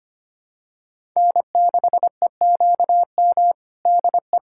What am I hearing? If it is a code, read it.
Morse code, heard as a keyed tone, about 25 wpm, 700 Hz. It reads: N6EQM DE
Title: USA Callsign DE Callsign 25wpm